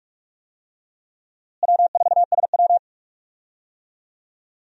Morse code, heard as a keyed tone, about 46 wpm, 700 Hz. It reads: W4SW
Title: USA Call Signs - 4 Characters - Repeated 3x - 45wpm